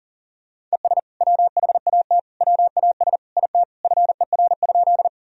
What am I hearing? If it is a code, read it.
Morse code, heard as a keyed tone, about 37 wpm, 700 Hz. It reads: ES WHAT WAS IT FER?
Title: Sentences from Top 100 Words and Abbreviations 40wpm